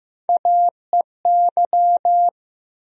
Morse code, heard as a keyed tone, about 15 wpm, 700 Hz. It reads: AEY
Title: USA Call Sign Postfixes - 3 Letters 15wpm